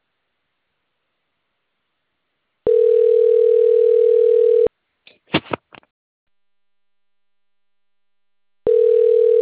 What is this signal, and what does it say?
Ringback tone (North America), 440 + 480 Hz.